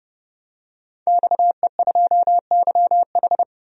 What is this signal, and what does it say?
Morse code, heard as a keyed tone, about 30 wpm, 700 Hz. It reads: XE2YH
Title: DXCC-100 Call Signs - 5 Characters - 30wpm